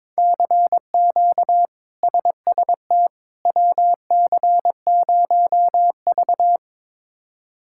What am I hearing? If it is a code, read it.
Morse code, heard as a keyed tone, about 22 wpm, 700 Hz. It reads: CQ SST WC0V
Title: K1USN SST QSO Practice - 22wpm